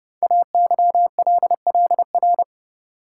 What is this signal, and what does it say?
Morse code, heard as a keyed tone, about 29 wpm, 700 Hz. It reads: AYLLR
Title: Code Groups - 5 Letters - 30wpm